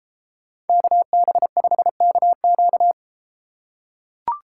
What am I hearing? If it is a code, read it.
Morse code, heard as a keyed tone, about 33 wpm, 700 Hz. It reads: KB5KQ